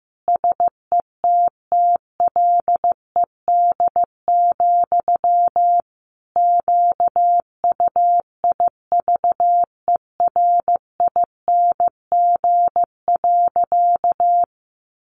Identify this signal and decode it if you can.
Morse code, heard as a keyed tone, about 15 wpm, 700 Hz. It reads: SETTLED, QUIVERING.
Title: A Sound Of Thunder 15wpm